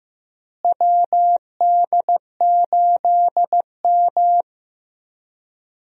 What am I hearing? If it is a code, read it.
Morse code, heard as a keyed tone, about 15 wpm, 700 Hz. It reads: WD8M
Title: USA Call Signs - 4 Characters - Repeated 3x - 15wpm